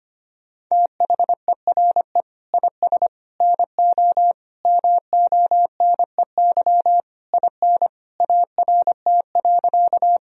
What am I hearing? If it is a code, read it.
Morse code, heard as a keyed tone, about 25 wpm, 700 Hz. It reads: THERE IS NO MONEY IN ART.
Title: Sentences from Top 1000 Words - Review Part 3 - 25wpm